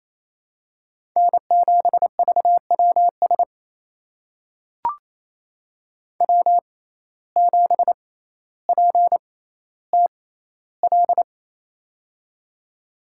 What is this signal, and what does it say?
Morse code, heard as a keyed tone, about 29 wpm, spaced as for 14 wpm, 700 Hz. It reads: N7VWS W7PTL
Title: USA Call Signs - 5 Characters - 6x Character Spacing - 28wpm